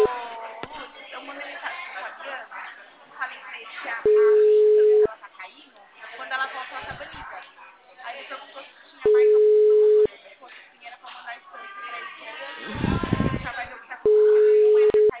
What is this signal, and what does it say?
Ringback tone (Europe, 425 Hz), with speech.
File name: out- -2010- - - .6